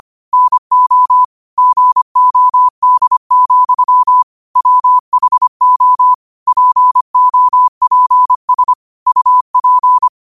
Morse code, heard as a keyed tone, about 25 wpm, 1000 Hz. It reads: NO GOD, WHO POPS UP